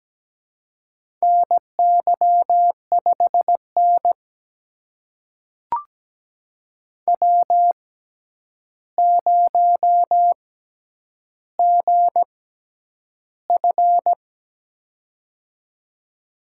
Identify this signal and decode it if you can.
Morse code, heard as a keyed tone, about 17 wpm, spaced as for 9 wpm, 700 Hz. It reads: NY5N W0GF
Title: USA Call Signs - 4 Characters - 6x Character Spacing - 17wpm